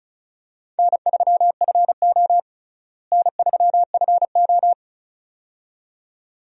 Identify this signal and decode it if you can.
Morse code, heard as a keyed tone, about 36 wpm, 700 Hz. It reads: N3FO N3FO
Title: USA Call Signs - 4 Characters - Repeated 2x - 35wpm